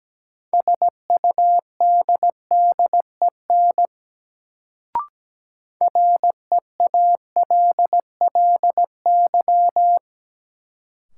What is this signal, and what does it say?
Morse code, heard as a keyed tone, about 17 wpm, 700 Hz. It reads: SUDDEN REALLY
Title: Sets of 4 Words - New Word Drill - 17wpm